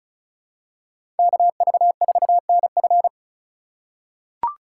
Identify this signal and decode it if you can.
Morse code, heard as a keyed tone, about 34 wpm, 700 Hz. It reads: KV4NF
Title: USA Call Signs - 5 Characters - 4x Character Spacing - 35wpm